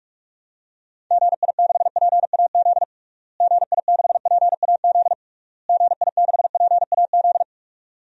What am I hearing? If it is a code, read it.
Morse code, heard as a keyed tone, about 46 wpm, 700 Hz. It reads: GI6PAZ GI6PAZ GI6PAZ
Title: DXCC-100 Call Signs - 6 Characters - Repeated 3x - 45wpm